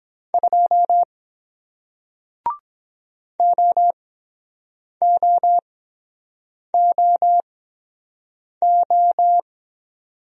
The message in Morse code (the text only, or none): 2 O O O O